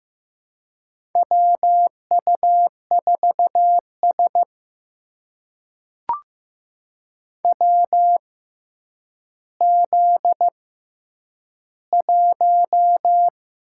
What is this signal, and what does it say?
Morse code, heard as a keyed tone, about 15 wpm, spaced as for 9 wpm, 700 Hz. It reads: WU4S WZ1
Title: USA Call Signs - 4 Characters - 6x Character Spacing - 15wpm